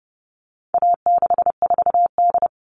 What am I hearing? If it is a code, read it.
Morse code, heard as a keyed tone, about 30 wpm, 700 Hz. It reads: A64D
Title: All Cut Numbers - 4-Digit 30wpm